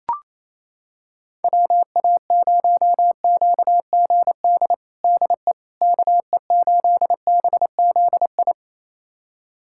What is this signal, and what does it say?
Morse code, heard as a keyed tone, about 28 wpm, 700 Hz. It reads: WA0QGD DE KE8BZI
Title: USA Callsign DE Callsign 28wpm